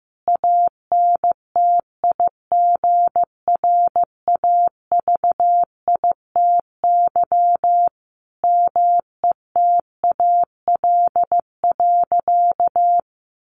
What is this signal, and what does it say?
Morse code, heard as a keyed tone, about 15 wpm, 700 Hz. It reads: ANTIGRAVITY METAL.
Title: A Sound Of Thunder 15wpm